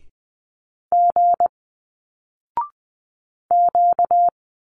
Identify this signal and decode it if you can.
Morse code, heard as a keyed tone, about 20 wpm, 700 Hz. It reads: G Q